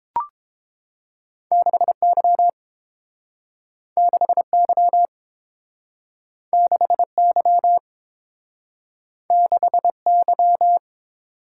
Morse code, heard as a keyed tone, about 30 wpm, 700 Hz. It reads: 6Y 6Y 6Y 6Y